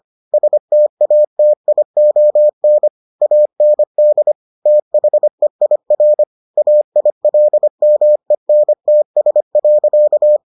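Morse code, heard as a keyed tone, about 25 wpm, 600 Hz. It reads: STATION AND THEIR AILMENTS.
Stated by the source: Morse edition of the Foundations of Amateur Radio episodes.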